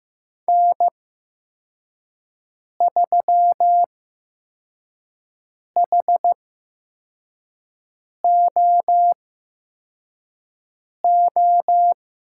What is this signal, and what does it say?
Morse code, heard as a keyed tone, about 15 wpm, spaced as for 4 wpm, 700 Hz. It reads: N3HOO
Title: USA Call Signs - 5 Characters - 8x Character Spacing - 15wpm